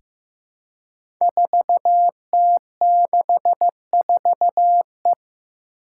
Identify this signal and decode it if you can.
Morse code, heard as a keyed tone, about 15 wpm, 700 Hz. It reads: 4T64E
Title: Five Challenging Letter-Number Combinations 15wpm